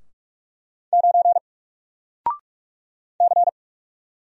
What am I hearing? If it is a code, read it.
Morse code, heard as a keyed tone, about 44 wpm, 700 Hz. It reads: 9 C
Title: Single Character (Letter:Number:Punctuation:Prosign) Alt 45wpm